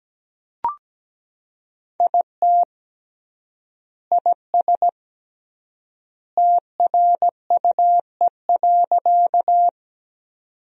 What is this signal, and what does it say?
Morse code, heard as a keyed tone, about 17 wpm, 700 Hz. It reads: IT IS TRUE.
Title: 3x Word-Spacing - Sentences from Top 400 Words - 17wpm